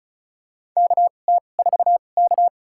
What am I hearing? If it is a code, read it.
Morse code, heard as a keyed tone, about 36 wpm, spaced as for 25 wpm, 700 Hz. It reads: KT4K
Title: USA Call Signs - 4 Characters - 2x Character Spacing - 35wpm